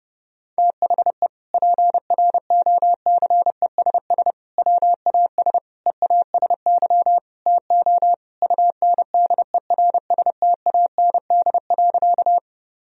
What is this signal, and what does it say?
Morse code, heard as a keyed tone, about 30 wpm, 700 Hz. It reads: THE PROCESS WAS EASY TO UNDERSTAND.
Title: Sentences from Top 1000 Words - Review Part 3 - 30wpm